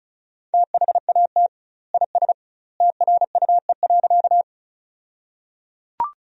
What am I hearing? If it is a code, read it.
Morse code, heard as a keyed tone, about 36 wpm, 700 Hz. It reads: THAT IS TRUE.
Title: 2x Word-Spacing - Sentences from Top 400 Words - 35wpm